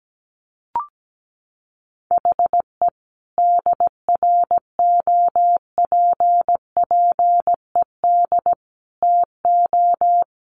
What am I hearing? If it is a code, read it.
Morse code, heard as a keyed tone, about 17 wpm, 700 Hz. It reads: HE DROPPED TO
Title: A Sound Of Thunder 17wpm